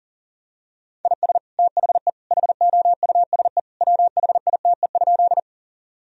Morse code, heard as a keyed tone, about 40 wpm, 700 Hz. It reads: IS THE HOUSE WHITE?
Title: Sentences from Top 1000 Words - Review Part 1 - 40wpm